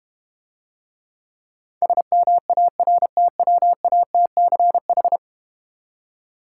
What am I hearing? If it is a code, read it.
Morse code, heard as a keyed tone, about 32 wpm, 700 Hz. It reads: SMARTWATCH